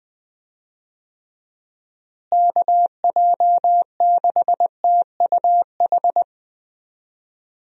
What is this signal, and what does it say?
Morse code, heard as a keyed tone, about 21 wpm, 700 Hz. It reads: KJ6TUH
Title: USA Call Signs - 6 Characters - Repeated 3x - 20wpm